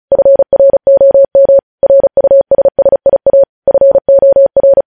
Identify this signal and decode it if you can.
Morse code, heard as a keyed tone, about 36 wpm, 550 Hz. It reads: FROM RUSSIA FOR
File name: podcast / News headlines-1x-35x35